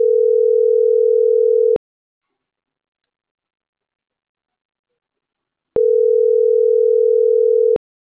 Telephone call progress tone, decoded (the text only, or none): ringback tone